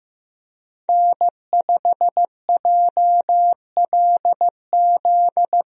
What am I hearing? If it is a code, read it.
Morse code, heard as a keyed tone, about 15 wpm, 700 Hz. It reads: N5JLZ
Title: USA Call Signs - 5 Characters - Repeated 2x - 15wpm